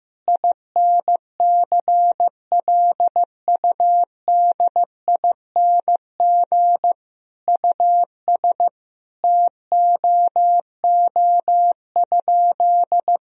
Morse code, heard as a keyed tone, about 15 wpm, 700 Hz. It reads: INCLUDING US TOO?
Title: Sentences from Top 1000 Words - Review Part 4 - 15wpm